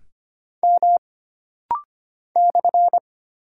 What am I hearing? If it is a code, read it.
Morse code, heard as a keyed tone, about 25 wpm, 700 Hz. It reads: M /
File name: Single Character (Letter:Number:Punctuation:Prosign) - Rapid-Fire - 25wpm